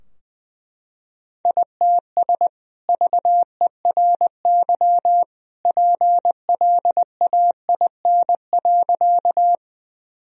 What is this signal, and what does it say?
Morse code, heard as a keyed tone, about 21 wpm, 700 Hz. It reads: ITS VERY PLAIN.
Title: Sentences from Top 1000 Words - Review Part 2 - 20wpm